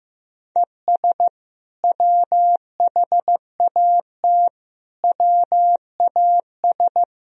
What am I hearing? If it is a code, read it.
Morse code, heard as a keyed tone, about 15 wpm, 700 Hz. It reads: ES WHAT WAS
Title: Sentences from Top 200 Words and Abbreviations 15wpm